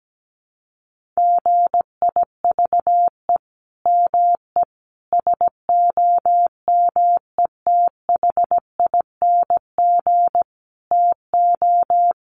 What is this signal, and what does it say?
Morse code, heard as a keyed tone, about 17 wpm, 700 Hz. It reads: GIVE ME SOMETHING TO
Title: Sentences from Top 1500 Words 17wpm